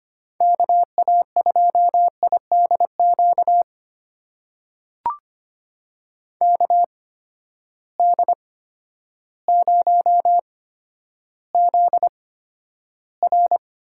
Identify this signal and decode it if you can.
Morse code, heard as a keyed tone, about 25 wpm, spaced as for 11 wpm, 700 Hz. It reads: KA2IDQ KD0ZR
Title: USA Call Signs - 6 Characters - 8x Character Spacing - 25wpm